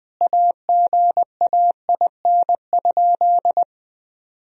Morse code, heard as a keyed tone, about 20 wpm, 700 Hz. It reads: AGAIN?
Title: Sentences from Top 1000 Words - Review Part 4 - 20wpm